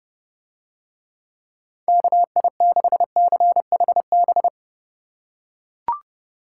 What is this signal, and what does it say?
Morse code, heard as a keyed tone, about 27 wpm, 700 Hz. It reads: KI6CHB